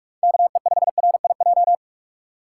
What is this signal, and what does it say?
Morse code, heard as a keyed tone, about 44 wpm, 700 Hz. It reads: KE5RIJ
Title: USA Call Signs - 6 Characters - 6x Character Spacing - 45wpm